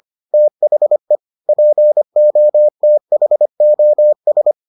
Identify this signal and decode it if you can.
Morse code, heard as a keyed tone, about 25 wpm, 600 Hz. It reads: THE POTHOS